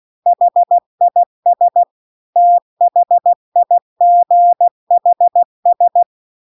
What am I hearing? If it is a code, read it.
Morse code, heard as a keyed tone, about 16 wpm, 700 Hz. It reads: HIS THIGHS
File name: CW-The-Big-Twoheaded-River-2-16WPM